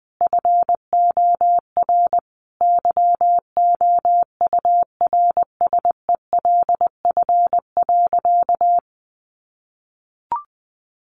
Morse code, heard as a keyed tone, about 20 wpm, 700 Hz. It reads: FOR YOURSELF.
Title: Sentences from Top 1400 Words 20wpm